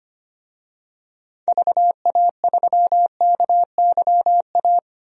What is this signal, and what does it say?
Morse code, heard as a keyed tone, about 25 wpm, 700 Hz. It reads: VA3KYA
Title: DXCC-100 Call Signs - 6 Characters - 25wpm